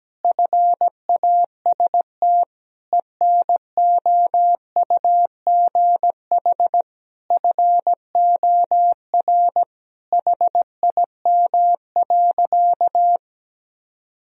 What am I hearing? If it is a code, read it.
Morse code, heard as a keyed tone, about 17 wpm, 700 Hz. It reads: FAST ENOUGH FOR HIM.
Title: Sentences from Top 1000 Words - Review Part 2 - 17wpm